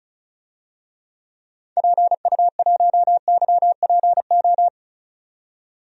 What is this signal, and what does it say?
Morse code, heard as a keyed tone, about 34 wpm, 700 Hz. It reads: PU1YPO
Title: DXCC-100 Call Signs - 6 Characters - 35wpm